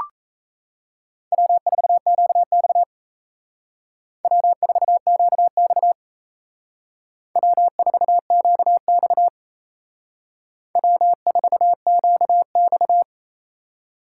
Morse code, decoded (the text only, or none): W4QX W4QX W4QX W4QX